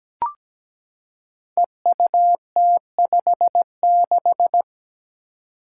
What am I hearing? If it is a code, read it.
Morse code, heard as a keyed tone, about 17 wpm, 700 Hz. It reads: EUT56
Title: Code Groups - 5 Letter:Numbers - 17wpm